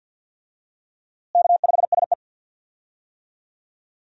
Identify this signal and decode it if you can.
Morse code, heard as a keyed tone, about 49 wpm, 700 Hz. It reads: K5SE
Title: USA Call Signs - 4 Characters - Repeated 3x - 50wpm